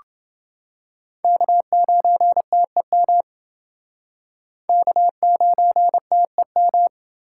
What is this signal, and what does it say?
Morse code, heard as a keyed tone, about 28 wpm, 700 Hz. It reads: K9TEM K9TEM